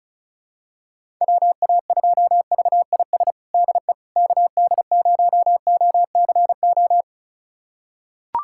Morse code, heard as a keyed tone, about 36 wpm, 700 Hz. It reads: WA2VIS DE KD0OCO
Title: USA Callsign DE Callsign - 6 Characters - 35wpm